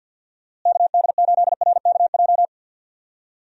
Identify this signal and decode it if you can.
Morse code, heard as a keyed tone, about 52 wpm, 700 Hz. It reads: KD8RKJ
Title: USA Call Signs - 6 Characters - Repeated 3x - 50wpm